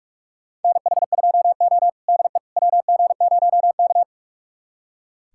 Morse code, heard as a keyed tone, about 44 wpm, 700 Hz. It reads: NH2O DE WG0K
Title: USA Callsign DE Callsign - 4 Characters - 45wpm